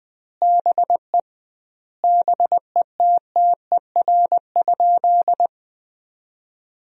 Morse code, heard as a keyed tone, about 20 wpm, 700 Hz. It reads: BE BETTER?
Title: 2x Word-Spacing - Sentences from Top 400 Words - 20wpm